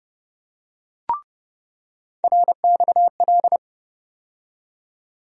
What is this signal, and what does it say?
Morse code, heard as a keyed tone, about 30 wpm, 700 Hz. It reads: RXL
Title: USA Call Sign Postfixes - 3 Letters 30wpm